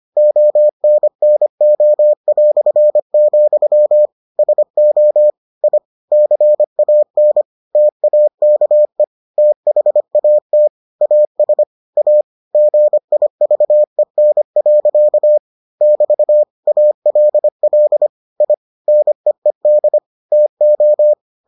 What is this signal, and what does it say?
Morse code, heard as a keyed tone, about 25 wpm, 600 Hz. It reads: ONNO", SO I CAN TAKE THAT AS A GIVEN. <BT> ALL I NEED TO